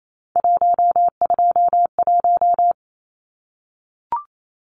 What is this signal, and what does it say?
Morse code, heard as a keyed tone, about 28 wpm, 700 Hz. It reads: 121
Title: Numbers - 3-Digit 28wpm